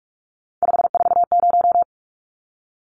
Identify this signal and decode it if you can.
Morse code, heard as a keyed tone, about 44 wpm, 700 Hz. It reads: 540
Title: Numbers - 3-Digit 45wpm